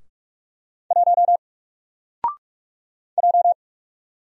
Morse code, heard as a keyed tone, about 45 wpm, 700 Hz. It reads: 1 J
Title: Single Letter-Number Alt 45wpm